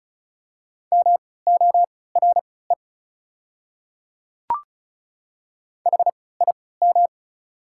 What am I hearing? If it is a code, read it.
Morse code, heard as a keyed tone, about 34 wpm, spaced as for 20 wpm, 690 Hz. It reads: MORE HIM
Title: 3x Character Spacing - Top 100 Words - 35wpm